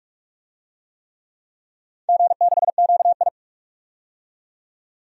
Morse code, heard as a keyed tone, about 46 wpm, 700 Hz. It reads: G6QI
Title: DXCC-100 Call Signs - 4 Characters - 45wpm